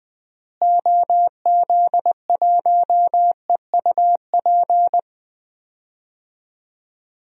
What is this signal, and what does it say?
Morse code, heard as a keyed tone, about 20 wpm, 700 Hz. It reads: OZ1EUP
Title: DXCC-100 Call Signs - 6 Characters - Repeated 2x - 20wpm